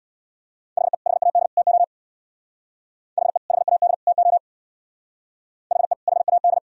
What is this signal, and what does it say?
Morse code, heard as a keyed tone, about 63 wpm, 700 Hz. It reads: HE HAD TO HE HAD TO HE HAD